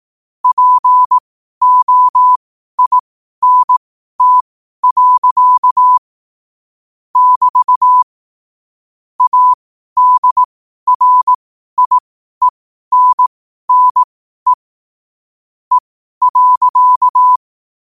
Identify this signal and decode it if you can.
Morse code, heard as a keyed tone, about 18 wpm, spaced as for 12 wpm, 1000 Hz. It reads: POINT. <BT> ADRIENNE E.